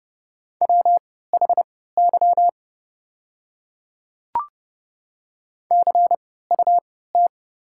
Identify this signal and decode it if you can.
Morse code, heard as a keyed tone, about 30 wpm, spaced as for 17 wpm, 700 Hz. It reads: WHY CUT
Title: 3x Character Spacing - Top 300 Words - 30wpm